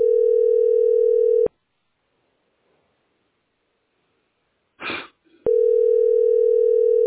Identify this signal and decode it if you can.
Ringback tone (North America), 440 + 480 Hz.